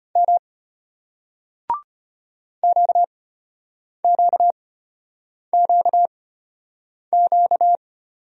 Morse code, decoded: M QQQQ

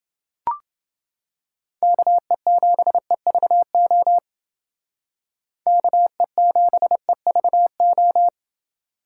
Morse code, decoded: KE7EVO KE7EVO

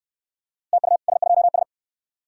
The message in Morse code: TU S2S

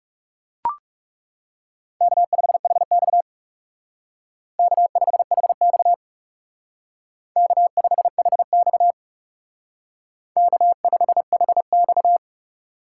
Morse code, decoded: K5HX K5HX K5HX K5HX

